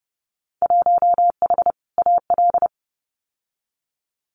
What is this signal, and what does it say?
Morse code, heard as a keyed tone, about 30 wpm, 700 Hz. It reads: 1H AL
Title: Winter Field Day Exchanges 30wpm